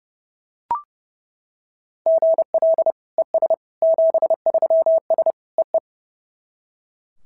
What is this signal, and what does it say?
Morse code, heard as a keyed tone, about 26 wpm, 650 Hz. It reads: GL ES 73S EE
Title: SOTA QSO Practice 30wpm